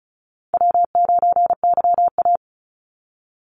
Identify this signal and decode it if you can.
Morse code, heard as a keyed tone, about 34 wpm, 700 Hz. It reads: W9YA
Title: Call Signs Repeated 2x - 35wpm